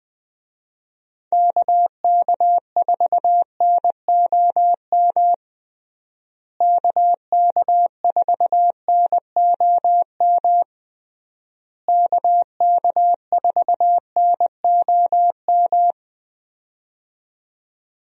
Morse code, heard as a keyed tone, about 20 wpm, 700 Hz. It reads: KK4NOM KK4NOM KK4NOM
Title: USA Call Signs - 6 Characters - Repeated 3x - 20wpm